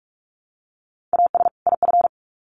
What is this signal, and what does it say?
Morse code, heard as a keyed tone, about 44 wpm, 700 Hz. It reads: AS IF
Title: Sets of 2 Words 45wpm